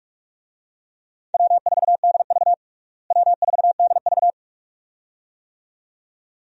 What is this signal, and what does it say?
Morse code, heard as a keyed tone, about 46 wpm, 700 Hz. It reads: W4DV W4DV
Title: USA Call Signs - 4 Characters - Repeated 2x - 45wpm